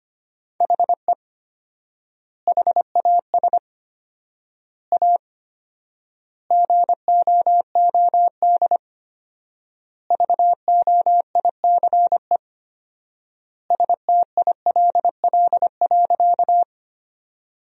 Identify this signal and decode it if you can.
Morse code, heard as a keyed tone, about 25 wpm, 700 Hz. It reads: HE HAS A GOOD VOICE STILL.
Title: 4x Word-Spacing - Sentences from Top 500 Words - 25wpm